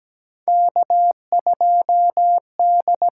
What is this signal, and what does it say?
Morse code, heard as a keyed tone, about 17 wpm, 700 Hz. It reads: K2D
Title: USA Call Signs - 4 Characters - 8x Character Spacing - 17wpm